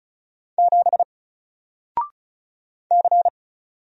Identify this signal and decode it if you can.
Morse code, heard as a keyed tone, about 36 wpm, 700 Hz. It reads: 7 C
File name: Single-Letter-Number-35wpm